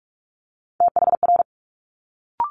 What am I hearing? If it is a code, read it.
Morse code, heard as a keyed tone, about 46 wpm, 700 Hz. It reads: THR
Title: Multi-letter Phonemes Quick Fire 45wpm